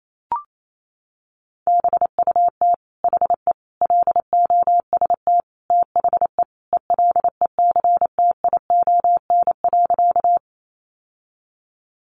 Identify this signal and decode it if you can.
Morse code, heard as a keyed tone, about 28 wpm, 700 Hz. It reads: BUT HE LOST THE ELECTION.
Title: Sentences from Top 1500 Words 28wpm